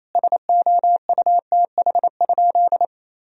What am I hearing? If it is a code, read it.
Morse code, heard as a keyed tone, about 27 wpm, 700 Hz. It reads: SOUTH?
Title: Sentences from Top 1000 Words - Review Part 2 - 28wpm